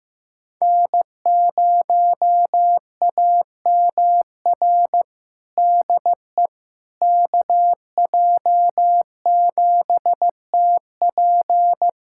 Morse code, heard as a keyed tone, about 15 wpm, 700 Hz. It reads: N0AMR DE KJ7TP
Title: USA Callsign DE Callsign - 5 Characters - 15wpm